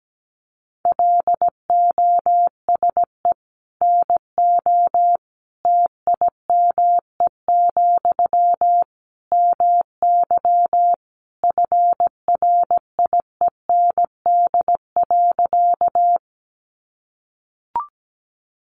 Morse code, heard as a keyed tone, about 17 wpm, 700 Hz. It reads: LOSE NO TIME, MY FRIEND.
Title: Sentences from Top 1100 Words 17wpm